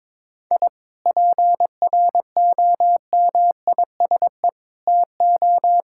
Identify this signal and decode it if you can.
Morse code, heard as a keyed tone, about 22 wpm, 700 Hz. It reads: I PROMISE TO
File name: Difficult Sentences with Hints - 22wpm